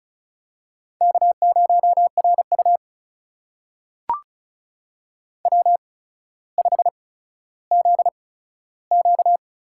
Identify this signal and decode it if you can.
Morse code, heard as a keyed tone, about 34 wpm, spaced as for 15 wpm, 700 Hz. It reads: K0RU W5ZQ
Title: USA Call Signs - 4 Characters - 8x Character Spacing - 35wpm